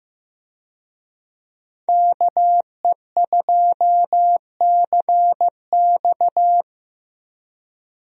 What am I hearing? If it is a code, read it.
Morse code, heard as a keyed tone, about 15 wpm, 700 Hz. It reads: KE2CX
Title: USA Call Signs - 5 Characters - Repeated 2x - 15wpm